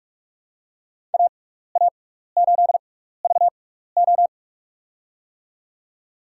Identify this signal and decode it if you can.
Morse code, heard as a keyed tone, about 47 wpm, spaced as for 16 wpm, 700 Hz. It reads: AA8VO
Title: USA Call Signs - 5 Characters - 6x Character Spacing - 45wpm